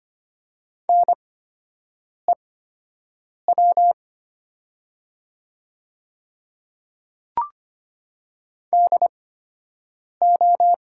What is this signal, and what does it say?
Morse code, heard as a keyed tone, about 25 wpm, spaced as for 7 wpm, 700 Hz. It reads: NEW DO
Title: 8x Character Spacing - Top 200 Words - 25wpm